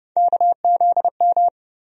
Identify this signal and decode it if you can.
Morse code, heard as a keyed tone, about 30 wpm, 700 Hz. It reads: KZM
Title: USA Call Sign Postfixes - 3 Letters 30wpm